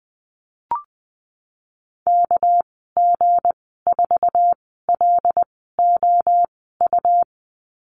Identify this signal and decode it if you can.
Morse code, heard as a keyed tone, about 20 wpm, spaced as for 15 wpm, 700 Hz. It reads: KG4LOU
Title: 219 Lesson 35g - Number 0 - Call Signs 20wpm